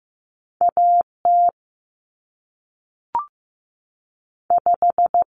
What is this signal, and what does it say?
Morse code, heard as a keyed tone, about 15 wpm, 700 Hz. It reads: A T 5
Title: Common Cut Numbers - 2-Digit 15wpm